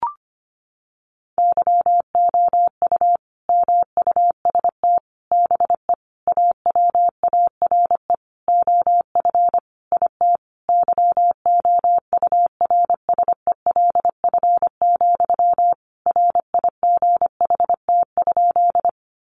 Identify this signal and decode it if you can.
Morse code, heard as a keyed tone, about 25 wpm, 700 Hz. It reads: YOU MUST BE AWARE OF IT YOURSELF, RIGHT?
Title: Sentences from Top 1200 Words 25wpm